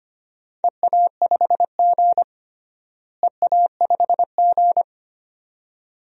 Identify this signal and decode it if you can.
Morse code, heard as a keyed tone, about 25 wpm, 700 Hz. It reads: EA5G EA5G
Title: DXCC-100 Call Signs - 4 Characters - Repeated 2x - 25wpm